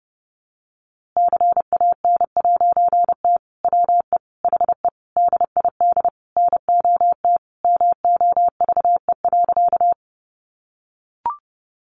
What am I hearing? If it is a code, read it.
Morse code, heard as a keyed tone, about 30 wpm, 700 Hz. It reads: CAN'T WE HE DID NOT MOVE.
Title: A Sound Of Thunder 30wpm